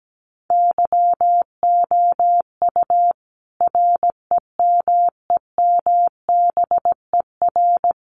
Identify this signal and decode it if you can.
Morse code, heard as a keyed tone, about 17 wpm, 700 Hz. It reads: YOU REMEMBER
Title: Sentences from Top 400 Words 17wpm